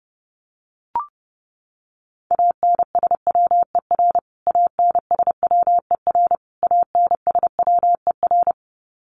Morse code, heard as a keyed tone, about 30 wpm, 700 Hz. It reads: ANSWER ANSWER ANSWER
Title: Top 300 Words Repeated 30wpm